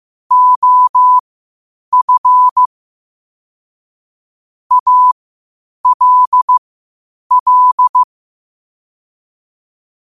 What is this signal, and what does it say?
Morse code, heard as a keyed tone, about 15 wpm, spaced as for 8 wpm, 1000 Hz. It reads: OF ALL
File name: qotdfiles / 8wpm